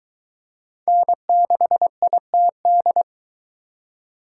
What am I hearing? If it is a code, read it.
Morse code, heard as a keyed tone, about 22 wpm, 700 Hz. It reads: N6ITD